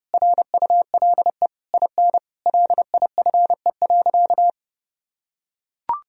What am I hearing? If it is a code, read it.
Morse code, heard as a keyed tone, about 30 wpm, 700 Hz. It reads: RULE IN LIFE.
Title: Sentences from Top 1000 Words - Review Part 2 - 30wpm